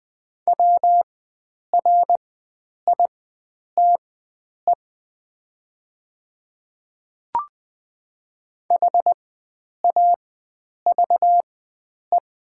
Morse code, heard as a keyed tone, about 20 wpm, spaced as for 9 wpm, 700 Hz. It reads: WRITE HAVE
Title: 4x Character Spacing - Top 100 Words - 20wpm